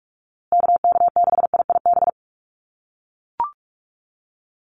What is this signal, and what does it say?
Morse code, heard as a keyed tone, about 44 wpm, 700 Hz. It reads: KK6IIB
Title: USA Call Signs Repeated 2x 45wpm